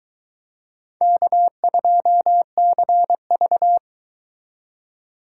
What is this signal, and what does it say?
Morse code, heard as a keyed tone, about 22 wpm, 700 Hz. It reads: K2CV